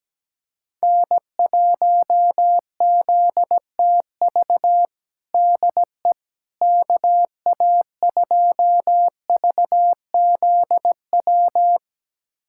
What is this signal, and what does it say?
Morse code, heard as a keyed tone, about 17 wpm, 700 Hz. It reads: N1ZTV DE KA2VZW
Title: USA Callsign DE Callsign 17wpm